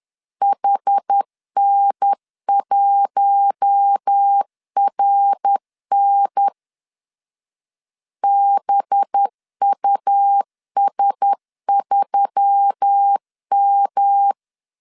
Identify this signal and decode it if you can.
Morse code, heard as a keyed tone, about 10 wpm, 800 Hz. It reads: HN1RN BUS3M
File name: CW / DET / 1 6 1500C S B 10wpm 4137